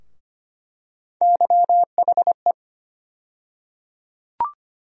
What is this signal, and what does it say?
Morse code, heard as a keyed tone, about 25 wpm, 700 Hz. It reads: YHE
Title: USA Call Sign Postfixes - 3 Letters 25wpm